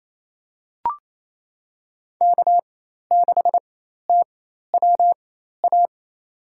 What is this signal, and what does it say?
Morse code, heard as a keyed tone, about 28 wpm, spaced as for 13 wpm, 700 Hz. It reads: K6TWA
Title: USA Call Signs - 5 Characters - 4x Character Spacing - 28wpm